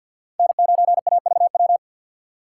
Morse code, heard as a keyed tone, about 53 wpm, 690 Hz. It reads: N9AVW
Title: USA Call Signs - 5 Characters - 3x Character Spacing - 50wpm